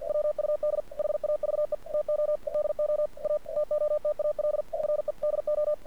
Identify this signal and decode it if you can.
Morse code, heard as a keyed tone, about 47 wpm, 600 Hz. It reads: WUD HAVE TO DO A TOTAL REDO
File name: archived / CRSnet